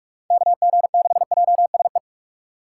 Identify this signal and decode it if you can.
Morse code, heard as a keyed tone, about 44 wpm, 690 Hz. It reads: KG6JSE
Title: USA Call Signs - 4x Character Spacing - 45wpm